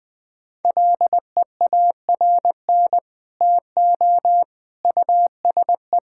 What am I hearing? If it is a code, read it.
Morse code, heard as a keyed tone, about 20 wpm, 700 Hz. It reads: LEARN TO USE
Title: Sentences with 2-Word Phrases - Spoken Beginning - 20wpm